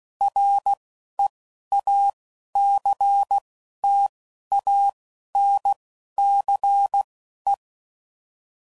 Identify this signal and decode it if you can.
Morse code, heard as a keyed tone, about 16 wpm, spaced as for 11 wpm, 800 Hz. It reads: REACTANCE
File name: CW Code Groups @10WPM / Technical Subjects / XC